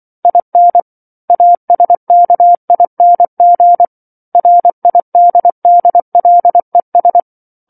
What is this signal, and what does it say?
Morse code, heard as a keyed tone, about 24 wpm, 700 Hz. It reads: IN ASKING RIDDLES